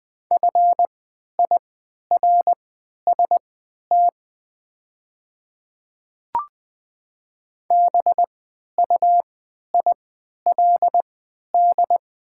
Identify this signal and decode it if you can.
Morse code, heard as a keyed tone, about 20 wpm, spaced as for 11 wpm, 690 Hz. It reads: FIRST BUILD
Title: 3x Character Spacing - Top 200 Words - 20wpm